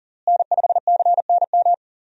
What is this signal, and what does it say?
Morse code, heard as a keyed tone, about 40 wpm, 700 Hz. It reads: N5CNM
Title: USA Call Signs - 5 Characters - Repeated 2x - 40wpm